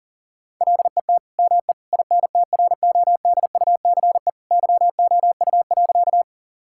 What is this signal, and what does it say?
Morse code, heard as a keyed tone, about 40 wpm, 700 Hz. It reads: LET ME INTRODUCE YOU.
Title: Sentences from Top 1600 Words 40wpm